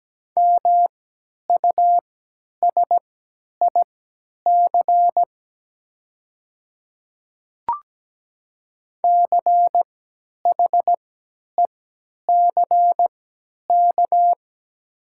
Morse code, heard as a keyed tone, about 17 wpm, spaced as for 10 wpm, 700 Hz. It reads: MUSIC CHECK
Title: 3x Character Spacing - Top 500 Words - 17wpm